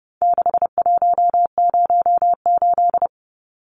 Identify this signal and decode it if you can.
Morse code, heard as a keyed tone, about 30 wpm, 700 Hz. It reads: 6108
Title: Numbers - 4-Digit 30wpm